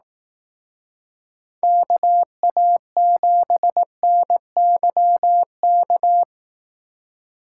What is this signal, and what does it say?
Morse code, heard as a keyed tone, about 18 wpm, 700 Hz. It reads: KA7NYK
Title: Speed Racing - USA 2x3 Call Signs 15wpm